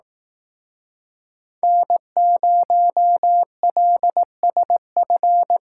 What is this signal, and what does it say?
Morse code, heard as a keyed tone, about 18 wpm, 700 Hz. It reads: N0LSF
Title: Speed Racing - USA 1x3 Call Signs 15wpm